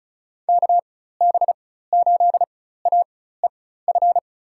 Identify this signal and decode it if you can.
Morse code, heard as a keyed tone, about 36 wpm, spaced as for 16 wpm, 700 Hz. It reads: KB8AEF
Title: USA Call Signs - 6 Characters - 4x Character Spacing - 35wpm